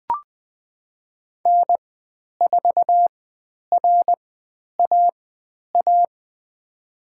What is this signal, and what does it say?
Morse code, heard as a keyed tone, about 20 wpm, spaced as for 10 wpm, 700 Hz. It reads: N4RAA
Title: 095 Lesson 16g - Letter c - Call Signs 20wpm